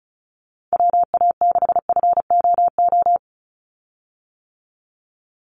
Morse code, heard as a keyed tone, about 36 wpm, 700 Hz. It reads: WA6FOO
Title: USA Call Signs 35wpm